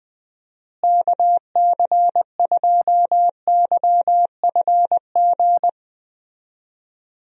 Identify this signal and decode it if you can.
Morse code, heard as a keyed tone, about 20 wpm, 700 Hz. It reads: KC2YFG
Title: USA Call Signs - 6 Characters - Repeated 2x - 20wpm